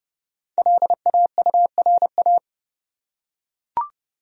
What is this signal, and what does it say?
Morse code, heard as a keyed tone, about 30 wpm, 700 Hz. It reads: LAURA
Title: USA Common Names - Max 5 Letters - 30wpm